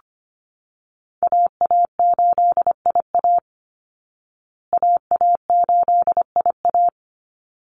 Morse code, heard as a keyed tone, about 25 wpm, 700 Hz. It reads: AA8IA AA8IA
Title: Call Signs Repeated 3x - 25wpm